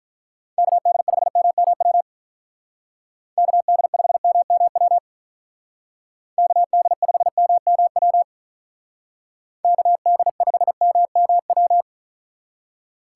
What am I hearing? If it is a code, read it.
Morse code, heard as a keyed tone, about 43 wpm, 700 Hz. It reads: KD5MMW KD5MMW KD5MMW KD5MMW
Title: Speed Racing - USA 2x3 Call Signs 35wpm